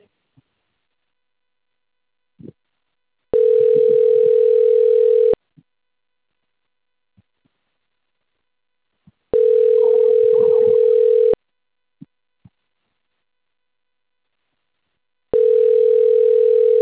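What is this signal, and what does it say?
Ringback tone (North America), 440 + 480 Hz.
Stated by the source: Outbound telephone call from the Cornelio Procopio call system.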